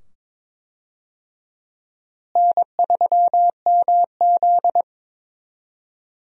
Morse code, heard as a keyed tone, about 22 wpm, 700 Hz. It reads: N3MZ
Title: USA Call Signs - 4 Characters - Repeated 2x - 22wpm